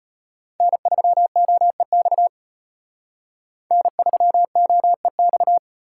Morse code, heard as a keyed tone, about 32 wpm, 700 Hz. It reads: N3OEX N3OEX